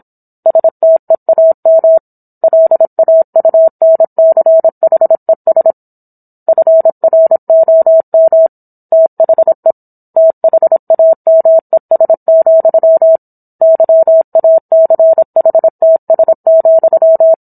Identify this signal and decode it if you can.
Morse code, heard as a keyed tone, about 27 wpm, 650 Hz. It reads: STEAM LAUNCHES FROM THE THAMES, YACHTS,